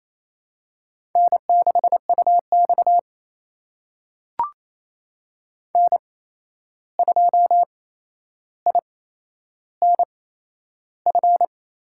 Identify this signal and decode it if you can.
Morse code, heard as a keyed tone, about 28 wpm, spaced as for 11 wpm, 700 Hz. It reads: N6UX N2INF
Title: USA Call Signs - 8x Character Spacing - 28wpm